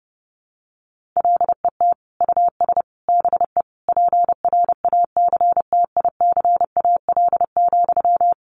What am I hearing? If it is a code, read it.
Morse code, heard as a keyed tone, about 30 wpm, 700 Hz. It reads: LET US BE PRACTICAL,
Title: Sentences from Top 1400 Words 30wpm